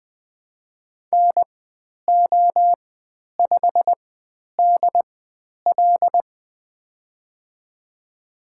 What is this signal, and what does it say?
Morse code, heard as a keyed tone, about 20 wpm, spaced as for 10 wpm, 700 Hz. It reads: NO5DL
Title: 095 Lesson 16g - Letter c - Call Signs 20wpm